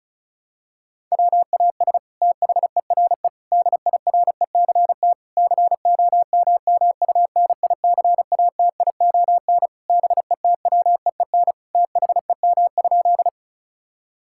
Morse code, heard as a keyed tone, about 36 wpm, 700 Hz. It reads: WAS THERE DIRECT COMMUNICATION BETWEEN THEM?
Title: Sentences from Top 1000 Words - Review Part 6 - 35wpm